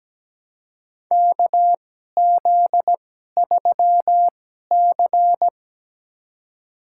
Morse code, heard as a keyed tone, about 17 wpm, spaced as for 12 wpm, 700 Hz. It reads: KZ3C
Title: USA Call Signs - 2x Character Spacing - 17wpm